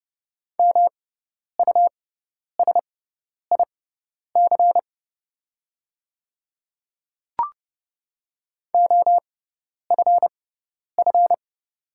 Morse code, heard as a keyed tone, about 30 wpm, spaced as for 10 wpm, 700 Hz. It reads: MUSIC OFF
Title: 6x Character Spacing - Top 300 Words - 30wpm